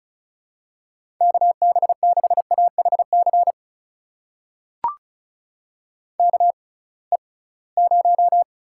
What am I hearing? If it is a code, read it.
Morse code, heard as a keyed tone, about 36 wpm, spaced as for 23 wpm, 700 Hz. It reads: KB6AHC KE0
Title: USA Call Signs - 6 Characters - 6x Character Spacing - 35wpm